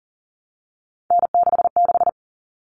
Morse code, heard as a keyed tone, about 40 wpm, 700 Hz. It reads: N66
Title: All Cut Numbers - 3-Digit 40wpm